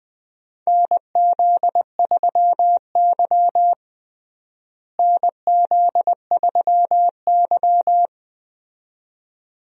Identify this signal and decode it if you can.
Morse code, heard as a keyed tone, about 21 wpm, 700 Hz. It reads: NZ3Y NZ3Y
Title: USA Call Signs - 4 Characters - Repeated 2x - 20wpm